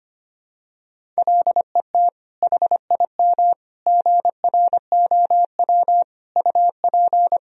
Morse code, heard as a keyed tone, about 25 wpm, 700 Hz. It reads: LET HIM GROW UP
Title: Sentences from Top 300 Words and Abbreviations 25wpm